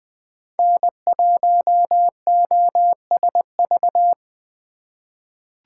Morse code, heard as a keyed tone, about 20 wpm, 700 Hz. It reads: N1OSV
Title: USA Call Signs - 5 Characters - 2x Character Spacing - 20wpm